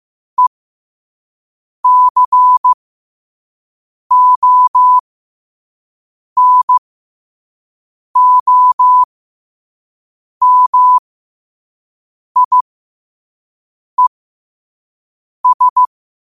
Morse code, heard as a keyed tone, about 15 wpm, spaced as for 5 wpm, 1000 Hz. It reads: ECONOMIES
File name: qotdfiles / 5wpm